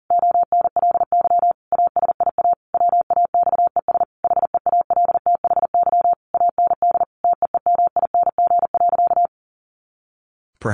Morse code, heard as a keyed tone, about 37 wpm, 700 Hz. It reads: ONLY ASIA WAXES HEALTHY AND TEEMING.
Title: A Sound Of Thunder 40wpm